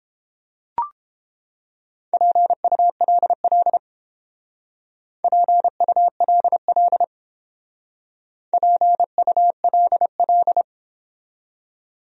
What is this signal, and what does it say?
Morse code, heard as a keyed tone, about 29 wpm, 700 Hz. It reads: PULL PULL PULL